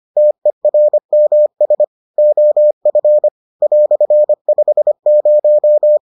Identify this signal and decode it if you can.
Morse code, heard as a keyed tone, about 25 wpm, 600 Hz. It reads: TERMS OF "50